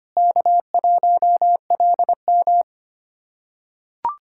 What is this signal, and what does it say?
Morse code, heard as a keyed tone, about 25 wpm, 700 Hz. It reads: K1LM
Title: USA Callsign DE Callsign - 4 Characters - 25wpm